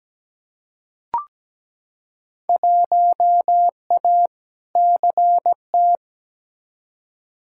Morse code, heard as a keyed tone, about 17 wpm, 700 Hz. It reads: 1A CT
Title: Field Day Exchanges Encore - 17wpm